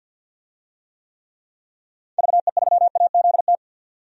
Morse code, heard as a keyed tone, about 51 wpm, 700 Hz. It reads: VE3AZT
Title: DXCC-100 Call Signs - 6 Characters - 50wpm